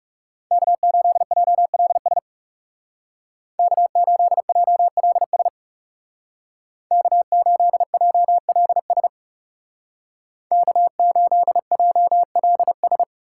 Morse code, heard as a keyed tone, about 36 wpm, 700 Hz. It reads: K8JLS K8JLS K8JLS K8JLS
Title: Speed Racing - USA Call Signs 30wpm